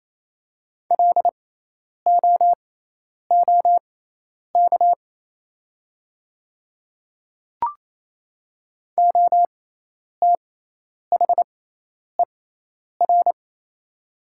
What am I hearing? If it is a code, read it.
Morse code, heard as a keyed tone, about 28 wpm, spaced as for 10 wpm, 700 Hz. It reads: LOOK OTHER
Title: 6x Character Spacing - Top 300 Words - 28wpm